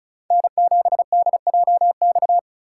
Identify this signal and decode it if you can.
Morse code, heard as a keyed tone, about 34 wpm, 700 Hz. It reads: N7DJX
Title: USA Call Signs - 5 Characters - 2x Character Spacing - 35wpm